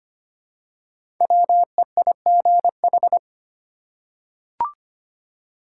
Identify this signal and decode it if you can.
Morse code, heard as a keyed tone, about 25 wpm, 700 Hz. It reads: WEIGH
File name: 5 Letter Words - First 3 Letters Given - 25wpm